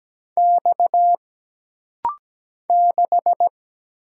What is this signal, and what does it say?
Morse code, heard as a keyed tone, about 17 wpm, 700 Hz. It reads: X 6
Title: Single Letter-Number - Rapid-Fire 17wpm